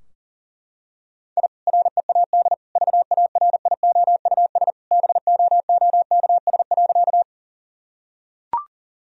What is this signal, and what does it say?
Morse code, heard as a keyed tone, about 40 wpm, 700 Hz. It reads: I READ VARIOUS BOOKS.
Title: Sentences from Top 1000 Words - Review Part 4 - 40wpm